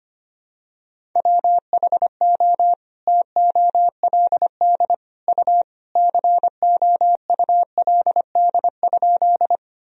Morse code, heard as a keyed tone, about 25 wpm, 700 Hz. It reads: WHO TOLD U COULD?
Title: Sentences from Top 400 Words and Abbreviations 25wpm